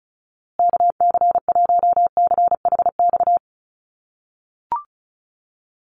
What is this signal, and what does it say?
Morse code, heard as a keyed tone, about 34 wpm, 700 Hz. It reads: KC1CHX
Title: USA Call Signs Repeated 3x 35wpm